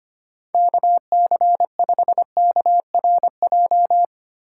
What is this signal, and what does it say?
Morse code, heard as a keyed tone, about 25 wpm, 700 Hz. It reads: KC5KRJ
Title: USA Call Signs - 6 Characters - 4x Character Spacing - 25wpm